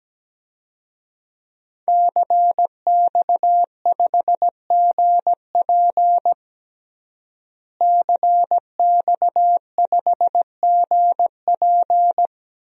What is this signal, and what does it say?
Morse code, heard as a keyed tone, about 17 wpm, 700 Hz. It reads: CX5GP CX5GP
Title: DXCC-100 Call Signs - 5 Characters - Repeated 2x - 17wpm